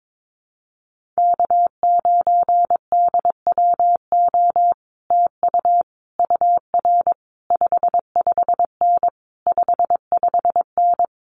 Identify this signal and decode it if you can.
Morse code, heard as a keyed tone, about 22 wpm, 700 Hz. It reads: K9DWO TU UR 55N 55N
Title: Common QSO Phrases Encore 22wpm